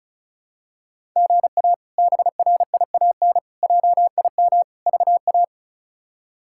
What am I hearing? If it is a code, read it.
Morse code, heard as a keyed tone, about 34 wpm, 700 Hz. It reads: GA BRIAN JIM VA
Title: K1USN SST QSO Practice - 35wpm